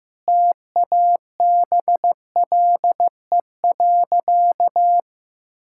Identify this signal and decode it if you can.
Morse code, heard as a keyed tone, about 15 wpm, 700 Hz. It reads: TABLE.
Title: Sentences from Top 1000 Words - Review Part 5 - 15wpm